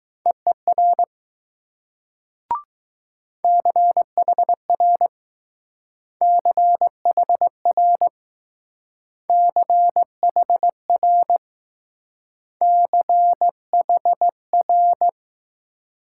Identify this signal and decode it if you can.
Morse code, heard as a keyed tone, about 20 wpm, 700 Hz. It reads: EER CHR CHR CHR CHR